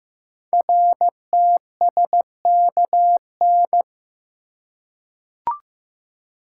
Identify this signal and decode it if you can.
Morse code, heard as a keyed tone, about 15 wpm, 700 Hz. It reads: RTSKN
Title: Code Groups - 5 Letters - 15wpm